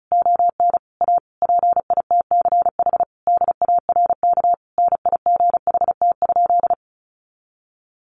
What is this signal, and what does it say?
Morse code, heard as a keyed tone, about 34 wpm, 700 Hz. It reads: ON A PITCH DARK NIGHT?
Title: Sentences from Top 1300 Words 35wpm